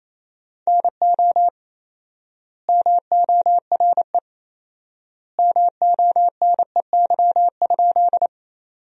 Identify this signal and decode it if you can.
Morse code, heard as a keyed tone, about 28 wpm, 700 Hz. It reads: NO MORE MONEY?
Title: 4x Word-Spacing - Sentences from Top 400 Words - 28wpm